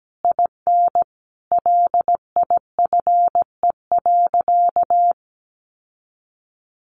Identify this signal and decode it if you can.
Morse code, heard as a keyed tone, about 17 wpm, 700 Hz. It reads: IN LIFE.
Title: Sentences from Top 1200 Words 17wpm